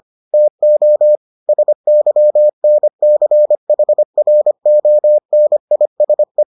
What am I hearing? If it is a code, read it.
Morse code, heard as a keyed tone, about 25 wpm, 600 Hz. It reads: TO SYNCHRONISE